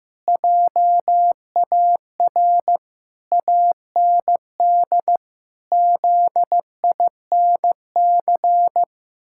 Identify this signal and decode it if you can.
Morse code, heard as a keyed tone, about 15 wpm, 700 Hz. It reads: JAR AND ZINC
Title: Pangram Challenge v2 - 15wpm